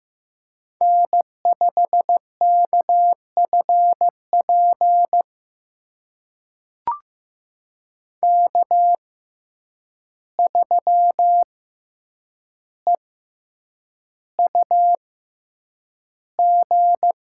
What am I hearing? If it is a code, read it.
Morse code, heard as a keyed tone, about 15 wpm, spaced as for 8 wpm, 700 Hz. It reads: N5KFP K3EUG
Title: USA Call Signs - 5 Characters - 6x Character Spacing - 15wpm